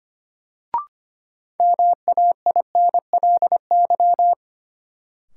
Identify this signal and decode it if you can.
Morse code, heard as a keyed tone, about 25 wpm, 700 Hz. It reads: MAINLY
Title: Sentences from Top 1400 Words - New Word Drill - 25wpm